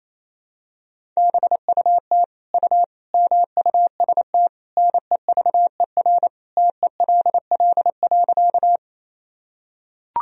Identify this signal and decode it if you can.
Morse code, heard as a keyed tone, about 28 wpm, 700 Hz. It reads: BUT U MUST NEVER TELL.
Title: Sentences from Top 300 Words and Abbreviations 28wpm